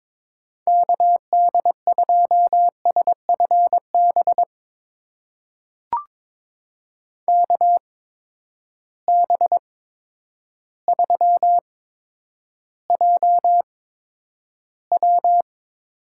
Morse code, heard as a keyed tone, about 22 wpm, spaced as for 10 wpm, 700 Hz. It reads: KD2SFB KB3JW
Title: USA Call Signs - 6 Characters - 8x Character Spacing - 22wpm